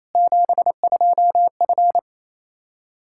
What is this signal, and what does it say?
Morse code, heard as a keyed tone, about 27 wpm, 700 Hz. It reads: 72F
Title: K1USN SST QSO Practice 28wpm